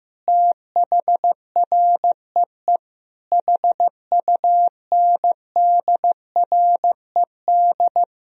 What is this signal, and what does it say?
Morse code, heard as a keyed tone, about 15 wpm, 700 Hz. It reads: THREE HUNDRED
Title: Sentences from Top 1000 Words - Review Part 2 - 15wpm